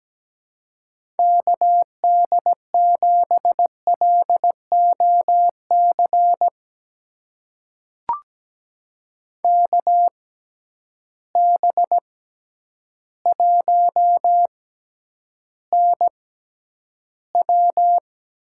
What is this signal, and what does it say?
Morse code, heard as a keyed tone, about 17 wpm, spaced as for 9 wpm, 700 Hz. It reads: KD7LOC KB1NW
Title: USA Call Signs - 6 Characters - 6x Character Spacing - 17wpm